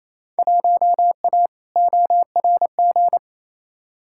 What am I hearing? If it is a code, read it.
Morse code, heard as a keyed tone, about 28 wpm, 700 Hz. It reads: 1A ORG
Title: Field Day Exchanges Encore - 28wpm